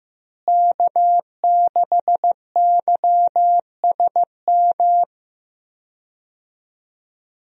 Morse code, heard as a keyed tone, about 15 wpm, 700 Hz. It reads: K6YSM
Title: USA Call Signs - 5 Characters - Repeated 3x - 15wpm